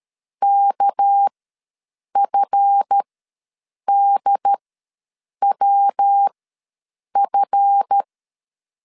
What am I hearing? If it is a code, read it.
Morse code, heard as a keyed tone, about 13 wpm, spaced as for 7 wpm, 800 Hz. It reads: KFDWF
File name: LEZIONI / 7 GF9Z / 7 GF ws 12wpm 3140